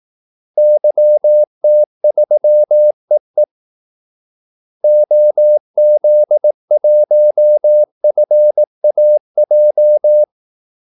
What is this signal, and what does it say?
Morse code, heard as a keyed tone, about 18 wpm, 600 Hz. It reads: YT3EE OZ1FAJ
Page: Kaldesignaler 18 wpm